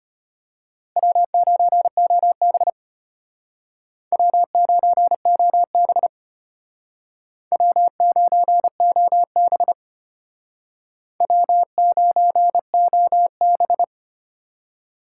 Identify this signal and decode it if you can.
Morse code, heard as a keyed tone, about 32 wpm, 700 Hz. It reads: W9OB W9OB W9OB W9OB